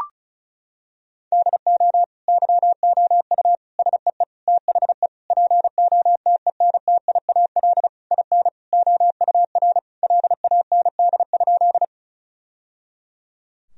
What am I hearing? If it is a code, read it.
Morse code, heard as a keyed tone, about 34 wpm, 700 Hz. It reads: DO YOU SEE THE POTENTIAL IN OUR LAND?
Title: Sentences from Top 1000 Words - Review Part 4 - 35wpm